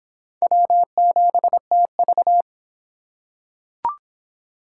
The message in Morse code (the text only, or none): W7TV